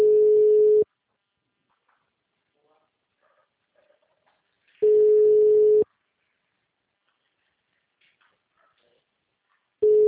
Ringback tone (Europe, 425 Hz).